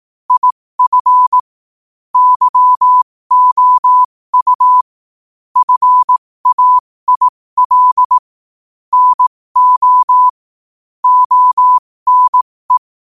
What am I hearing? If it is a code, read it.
Morse code, heard as a keyed tone, about 18 wpm, spaced as for 15 wpm, 1000 Hz. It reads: IF YOU FAIL NO ONE
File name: qotdfiles / 15wpm